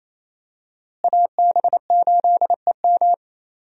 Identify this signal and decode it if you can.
Morse code, heard as a keyed tone, about 28 wpm, 700 Hz. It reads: AB8EM
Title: USA Call Signs - 5 Characters - Repeated 2x - 28wpm